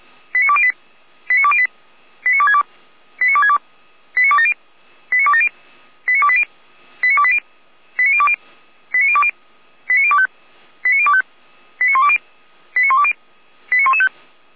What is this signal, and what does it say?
Five-tone selective call (ZVEI), heard as quick runs of five tones, tones of about 70 ms. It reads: 89208 89208 89262 89262 89270 89270 89280 89280 89020 89020 89025 89025 89120 89120 89115